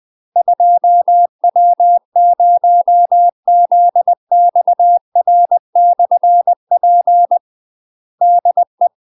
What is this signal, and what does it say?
Morse code, heard as a keyed tone, about 20 wpm, 700 Hz. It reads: 2W0ZXR/P DE
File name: UK MASTER QSOs TRAINING / 20WPM / UK MASTER QSOs 0019